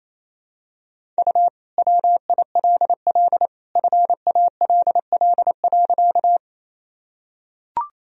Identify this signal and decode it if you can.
Morse code, heard as a keyed tone, about 28 wpm, 700 Hz. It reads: U WILL FALL.
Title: Sentences from Top 500 Words and Abbreviations 28wpm